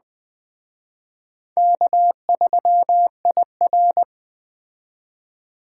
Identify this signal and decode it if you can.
Morse code, heard as a keyed tone, about 20 wpm, 700 Hz. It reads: K3IR